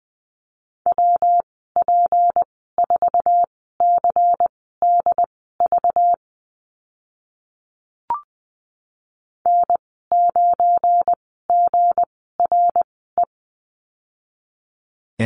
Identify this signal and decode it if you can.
Morse code, heard as a keyed tone, about 20 wpm, spaced as for 15 wpm, 700 Hz. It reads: WP4CDV N9GRE
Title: 193 Lesson 31g - Letter v - Call Signs 20wpm